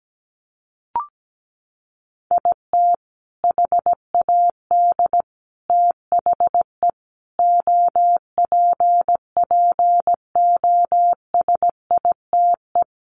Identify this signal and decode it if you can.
Morse code, heard as a keyed tone, about 17 wpm, 700 Hz. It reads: IT HAD THE OPPOSITE
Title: Sentences from Top 1300 Words 17wpm